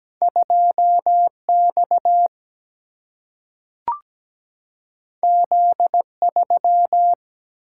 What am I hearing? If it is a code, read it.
Morse code, heard as a keyed tone, about 17 wpm, 700 Hz. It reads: 2X Z3
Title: Single-Letter Single-Number 17wpm